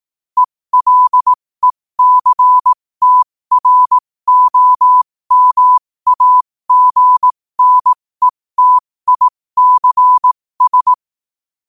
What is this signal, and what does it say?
Morse code, heard as a keyed tone, about 18 wpm, spaced as for 15 wpm, 1000 Hz. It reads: ELECTROMAGNETICS
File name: qotdfiles / 15wpm